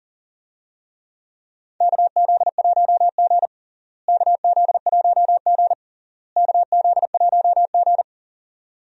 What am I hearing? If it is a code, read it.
Morse code, heard as a keyed tone, about 40 wpm, 700 Hz. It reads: KZ1G KZ1G KZ1G
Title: USA Call Signs - 4 Characters - Repeated 3x - 40wpm